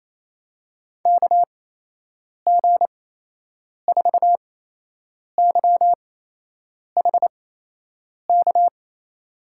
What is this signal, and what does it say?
Morse code, heard as a keyed tone, about 27 wpm, spaced as for 8 wpm, 700 Hz. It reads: KG4YHK
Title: USA Call Signs - 6 Characters - 8x Character Spacing - 28wpm